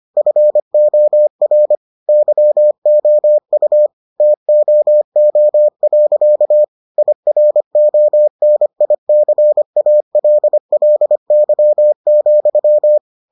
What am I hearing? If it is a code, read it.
Morse code, heard as a keyed tone, about 25 wpm, 600 Hz. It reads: FOR YOU TOO. IRONICALLY,